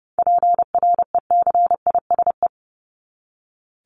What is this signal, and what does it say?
Morse code, heard as a keyed tone, about 30 wpm, 700 Hz. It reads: PRECISE
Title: LY Suffix - Top 100 Words with Base Words 30wpm